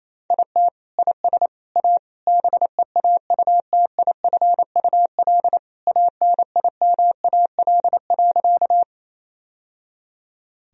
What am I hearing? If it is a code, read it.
Morse code, heard as a keyed tone, about 28 wpm, 700 Hz. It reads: IT IS A BEAUTIFUL ANIMAL.
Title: Sentences from Top 1000 Words - Review Part 1 - 28wpm